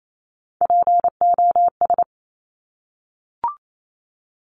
Morse code, heard as a keyed tone, about 28 wpm, 700 Hz. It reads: POS
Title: Top Three Letter Combinations in Words 28wpm